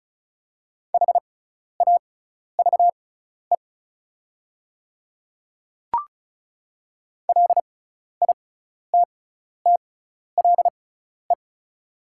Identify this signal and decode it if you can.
Morse code, heard as a keyed tone, about 34 wpm, spaced as for 12 wpm, 700 Hz. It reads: HAVE LITTLE
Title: 6x Character Spacing - Top 200 Words - 35wpm